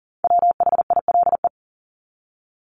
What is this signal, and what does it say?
Morse code, heard as a keyed tone, about 39 wpm, 700 Hz. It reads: WHILE
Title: Top 300 Words Repeated 40wpm